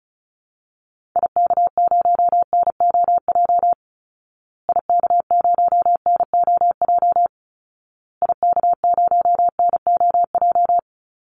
Morse code, heard as a keyed tone, about 34 wpm, 700 Hz. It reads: IK0NOJ IK0NOJ IK0NOJ
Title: Call Signs Repeated 2x - 35wpm